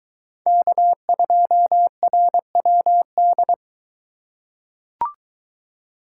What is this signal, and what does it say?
Morse code, heard as a keyed tone, about 20 wpm, 700 Hz. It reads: K2RWD